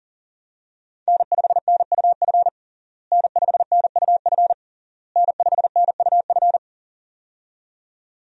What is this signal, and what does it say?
Morse code, heard as a keyed tone, about 40 wpm, 700 Hz. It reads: N5NUF N5NUF N5NUF
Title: USA Call Signs - 5 Characters - Repeated 3x - 40wpm